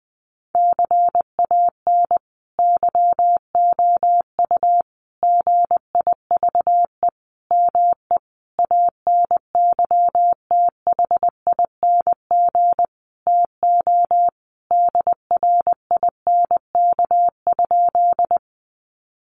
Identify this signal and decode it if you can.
Morse code, heard as a keyed tone, about 20 wpm, 700 Hz. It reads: CAN YOU GIVE ME ANYTHING TO DRINK?
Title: Sentences from Top 1500 Words 20wpm